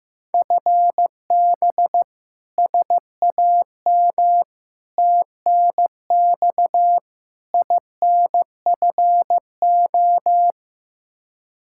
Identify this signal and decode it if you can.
Morse code, heard as a keyed tone, about 15 wpm, 700 Hz. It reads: FB SAM TNX INFO
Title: Practice QSO - Exchange 3 - More Info - 15wpm